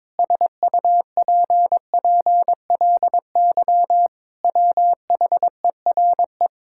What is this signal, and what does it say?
Morse code, heard as a keyed tone, about 22 wpm, 700 Hz. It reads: SUPPLY WHERE
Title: Sentences from Top 1000 Words - Review Part 5 - 22wpm